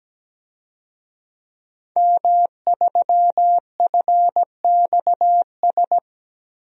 Morse code, heard as a keyed tone, about 17 wpm, 700 Hz. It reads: M3FXS
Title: DXCC-100 Call Signs - 5 Characters - Repeated 2x - 17wpm